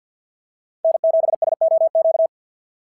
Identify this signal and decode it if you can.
Morse code, heard as a keyed tone, about 52 wpm, 690 Hz. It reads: N7SOQ
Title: SOTA QSO Practice 50wpm